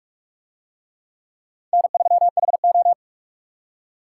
Morse code, heard as a keyed tone, about 45 wpm, 700 Hz. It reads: N3HO